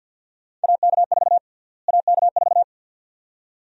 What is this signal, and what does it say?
Morse code, heard as a keyed tone, about 46 wpm, 700 Hz. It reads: AK4 AK4
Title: USA Call Sign Prefixes - Repeated 2x - 50wpm